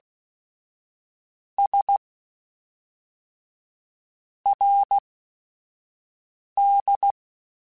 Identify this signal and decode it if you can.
Morse code, heard as a keyed tone, about 16 wpm, spaced as for 4 wpm, 800 Hz. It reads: SRD
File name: QSOs / QSOs-mp3s / WA7QOX 1@5